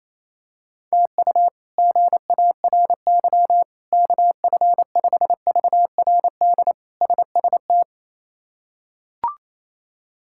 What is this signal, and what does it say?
Morse code, heard as a keyed tone, about 28 wpm, 700 Hz. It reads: TU GARY KF5VRD SST
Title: K1USN SST QSO Practice - 28wpm